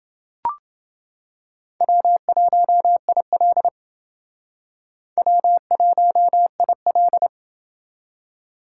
Morse code, heard as a keyed tone, about 28 wpm, 700 Hz. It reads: W1IL W1IL